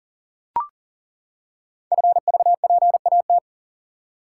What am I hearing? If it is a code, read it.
Morse code, heard as a keyed tone, about 39 wpm, 700 Hz. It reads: FVPAT
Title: Code Groups - 5 Letters - 40wpm